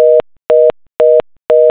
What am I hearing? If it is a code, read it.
Reorder tone (North America), 480 + 620 Hz, with noise.
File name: Network Busy